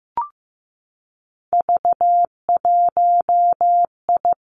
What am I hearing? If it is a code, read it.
Morse code, heard as a keyed tone, about 15 wpm, 700 Hz. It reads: V1I
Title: Three Challenging Letter-Number Combinations 15wpm